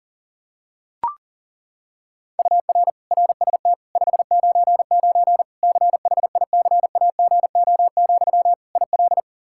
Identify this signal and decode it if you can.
Morse code, heard as a keyed tone, about 37 wpm, 700 Hz. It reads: UR RST 599 CHICAGO, IL
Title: Practice QSO - Exchange 1 - Beginning - 40wpm